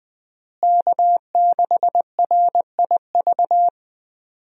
Morse code, heard as a keyed tone, about 20 wpm, 700 Hz. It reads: K6RIV
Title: USA Call Signs - 5 Characters - Repeated 3x - 20wpm